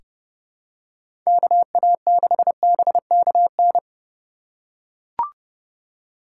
Morse code, heard as a keyed tone, about 30 wpm, 700 Hz. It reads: KA6BKN
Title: USA Call Signs - 2x Character Spacing - 30wpm